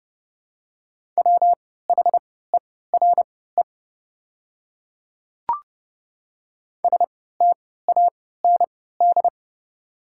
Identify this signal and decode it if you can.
Morse code, heard as a keyed tone, about 30 wpm, spaced as for 17 wpm, 700 Hz. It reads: WHERE STAND
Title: 3x Character Spacing - Top 300 Words - 30wpm